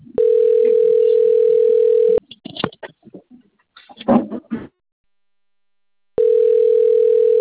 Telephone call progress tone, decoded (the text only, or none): ringback tone